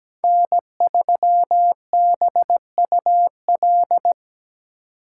Morse code, heard as a keyed tone, about 17 wpm, 700 Hz. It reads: N3BUL
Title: USA Call Signs - 5 Characters - 4x Character Spacing - 17wpm